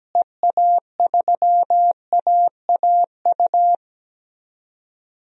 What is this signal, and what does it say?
Morse code, heard as a keyed tone, about 17 wpm, 700 Hz. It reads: EA3AAU
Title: DXCC-100 Call Signs - 6 Characters - Repeated 3x - 17wpm